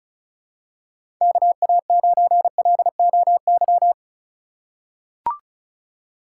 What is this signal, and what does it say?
Morse code, heard as a keyed tone, about 36 wpm, 700 Hz. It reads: KA9LOY
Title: USA Call Signs - 2x Character Spacing - 35wpm